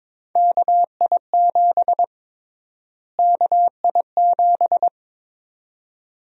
Morse code, heard as a keyed tone, about 22 wpm, 700 Hz. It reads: KI7 KI7
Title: USA Call Sign Prefixes - Repeated 2x - 22wpm